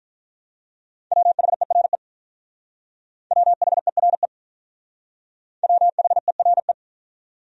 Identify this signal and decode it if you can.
Morse code, heard as a keyed tone, about 45 wpm, 700 Hz. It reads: WHERE WHERE WHERE